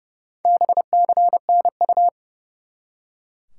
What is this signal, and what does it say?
Morse code, heard as a keyed tone, about 30 wpm, 700 Hz. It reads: BCNU
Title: Top 90 QSO Elements 30wpm